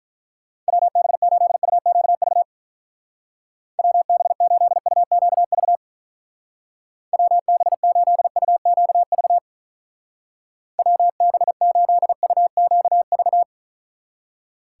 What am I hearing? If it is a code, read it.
Morse code, heard as a keyed tone, about 44 wpm, 700 Hz. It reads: WB8UQV WB8UQV WB8UQV WB8UQV
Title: Speed Racing - USA 2x3 Call Signs 35wpm